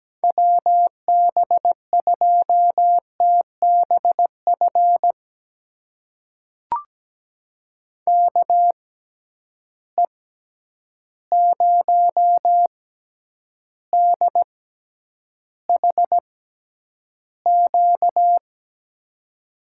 Morse code, heard as a keyed tone, about 17 wpm, spaced as for 9 wpm, 700 Hz. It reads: WB2TBF KE0DHQ
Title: USA Call Signs - 6x Character Spacing - 17wpm